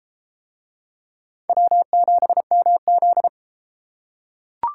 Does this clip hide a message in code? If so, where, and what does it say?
Yes: Morse code from 1.5 to 3.3 s, heard as a keyed tone, about 30 wpm, 700 Hz. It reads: W7MZ